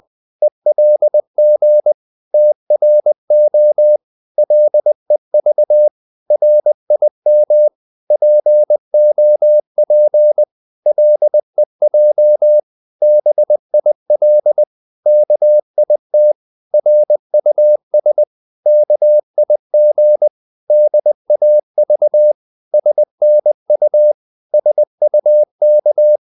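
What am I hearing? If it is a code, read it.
Morse code, heard as a keyed tone, about 20 wpm, 600 Hz. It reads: ELG TRO LEV RIM POP LEJ BIL KIT RUS KIG DAV SNU SUK
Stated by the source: Morse code practice of short Danish words at 20 wpm, from CW med Gnister.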